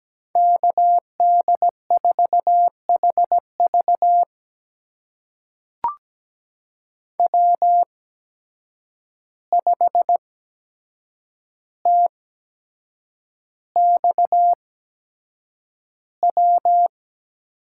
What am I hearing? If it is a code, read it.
Morse code, heard as a keyed tone, about 17 wpm, spaced as for 7 wpm, 700 Hz. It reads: KD4HV W5TXW
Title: USA Call Signs - 5 Characters - 8x Character Spacing - 17wpm